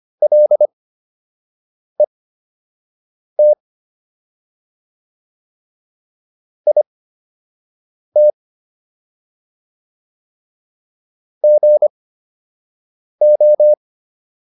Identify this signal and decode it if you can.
Morse code, heard as a keyed tone, about 25 wpm, spaced as for 6 wpm, 600 Hz. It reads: LET IT GO